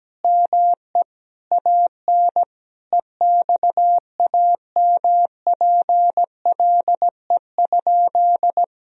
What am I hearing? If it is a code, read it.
Morse code, heard as a keyed tone, about 17 wpm, 700 Hz. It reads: ME AN EXAMPLE?
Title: Sentences from Top 1000 Words - Review Part 1 - 17wpm